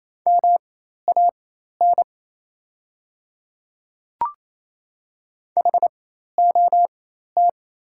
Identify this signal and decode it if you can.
Morse code, heard as a keyed tone, about 27 wpm, spaced as for 13 wpm, 690 Hz. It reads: MAN HOT
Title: 4x Character Spacing - Top 200 Words - 28wpm